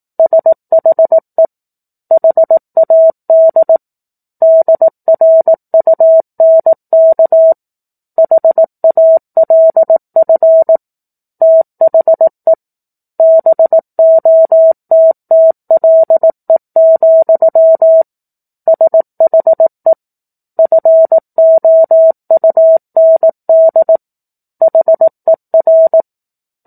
Morse code, heard as a keyed tone, about 18 wpm, 650 Hz. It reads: SHE HAD DRUNK HALF THE BOTTLE, SHE FOUND HER
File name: 04-Chapter-04 18 WPM